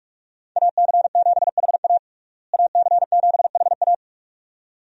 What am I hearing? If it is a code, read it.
Morse code, heard as a keyed tone, about 43 wpm, 700 Hz. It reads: AC7HA AC7HA
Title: USA Call Signs - 5 Characters - Repeated 2x - 45wpm